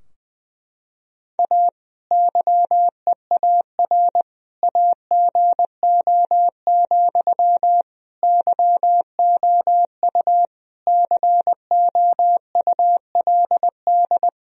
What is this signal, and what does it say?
Morse code, heard as a keyed tone, about 21 wpm, 700 Hz. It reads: A YEAR AGO, YOU COULD
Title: Sentences from Top 1000 Words - Review Part 3 - 20wpm